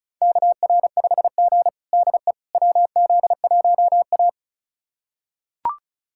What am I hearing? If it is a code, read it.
Morse code, heard as a keyed tone, about 34 wpm, 700 Hz. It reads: KR5G DE WZ1A
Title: USA Callsign DE Callsign - 4 Characters - 35wpm